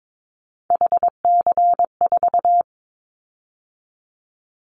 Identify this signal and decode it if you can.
Morse code, heard as a keyed tone, about 22 wpm, 700 Hz. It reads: HC4
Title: Callsign Prefixes for Top 100 DXCC - 22wpm